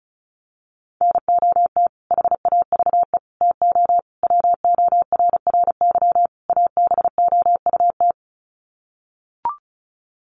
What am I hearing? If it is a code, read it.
Morse code, heard as a keyed tone, about 34 wpm, 700 Hz. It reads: NOT HAVE TO WORRY ABOUT
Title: Sets of 5 Words 35wpm